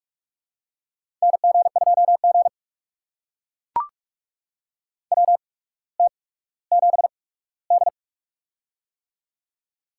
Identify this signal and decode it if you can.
Morse code, heard as a keyed tone, about 44 wpm, spaced as for 19 wpm, 700 Hz. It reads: NG2G WT7D
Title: USA Call Signs - 4 Characters - 8x Character Spacing - 45wpm